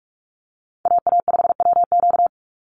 Morse code, heard as a keyed tone, about 44 wpm, 700 Hz. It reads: AA5WQ
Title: USA Call Signs 45wpm